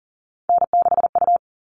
Morse code, heard as a keyed tone, about 40 wpm, 700 Hz. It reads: N6U
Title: All Cut Numbers - 3-Digit 40wpm